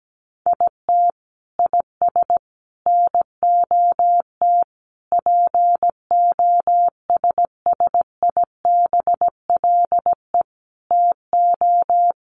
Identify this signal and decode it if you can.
Morse code, heard as a keyed tone, about 17 wpm, 700 Hz. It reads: IT IS NOT POSSIBLE TO
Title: Sets of 5 Words Encore2 17wpm